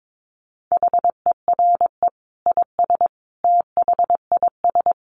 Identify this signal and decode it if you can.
Morse code, heard as a keyed tone, about 22 wpm, 700 Hz. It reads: HERE IS THIS
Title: Sentences from Top 1000 Words 22wpm